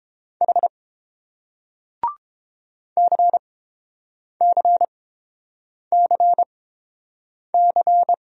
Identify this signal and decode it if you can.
Morse code, heard as a keyed tone, about 30 wpm, spaced as for 7 wpm, 700 Hz. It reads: H CCCC